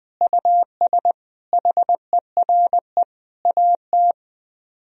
Morse code, heard as a keyed tone, about 20 wpm, 700 Hz. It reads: US HERE AT